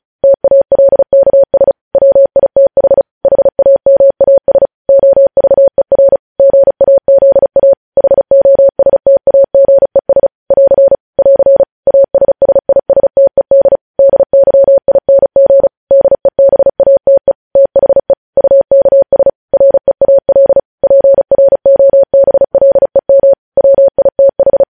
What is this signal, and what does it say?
Morse code, heard as a keyed tone, about 36 wpm, 550 Hz. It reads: TALKS WITH HAMAS OVER GAZA HOSTAGES <AR> <AR> ASSISTED DYING DEBATE THE UKS REAL PROBLEM WITH
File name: podcast / News headlines-1x-35x35